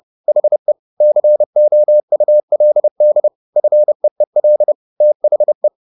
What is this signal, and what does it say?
Morse code, heard as a keyed tone, about 30 wpm, 600 Hz. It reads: HE COULD FEEL THE
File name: the old man and the sea / the old man and the sea 1 30wpm